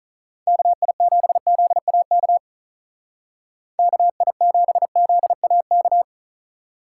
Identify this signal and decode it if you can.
Morse code, heard as a keyed tone, about 43 wpm, 700 Hz. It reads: KI7ZAK KI7ZAK
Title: Speed Racing - USA 2x3 Call Signs 35wpm